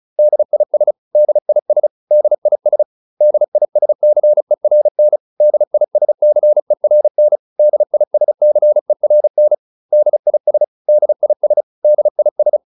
Morse code, heard as a keyed tone, about 34 wpm, 600 Hz. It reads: DIS DIS DIS DISCERN DISCERN DISCERN DIS DIS DIS